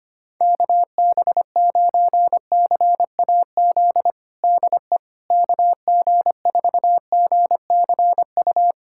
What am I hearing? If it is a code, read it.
Morse code, heard as a keyed tone, about 25 wpm, 700 Hz. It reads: KB9CAZ DE KG4GCU
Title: USA Callsign DE Callsign 25wpm